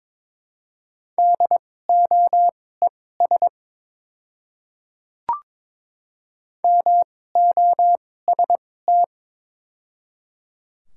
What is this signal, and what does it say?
Morse code, heard as a keyed tone, about 22 wpm, spaced as for 16 wpm, 700 Hz. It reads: DOES MOST
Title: 2x Character Spacing - Top 200 Words - 22wpm